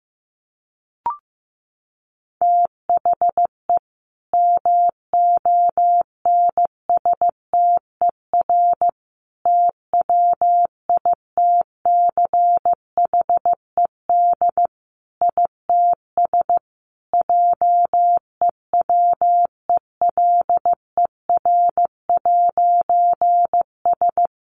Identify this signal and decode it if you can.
Morse code, heard as a keyed tone, about 15 wpm, 700 Hz. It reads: THE MONSTER TWITCHED ITS JEWELER'S
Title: A Sound Of Thunder 15wpm